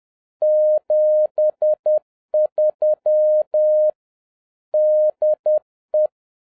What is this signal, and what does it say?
Morse code, heard as a keyed tone, about 10 wpm, 610 Hz. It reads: 73 DE
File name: 2011-07-mp3 / n21 GREETINGS FROM SINGAPORE 10wpm